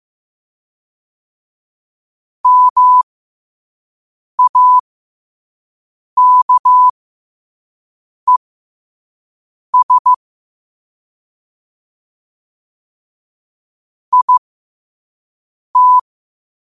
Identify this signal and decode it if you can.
Morse code, heard as a keyed tone, about 15 wpm, spaced as for 5 wpm, 1000 Hz. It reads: MAKES IT
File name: qotdfiles / 5wpm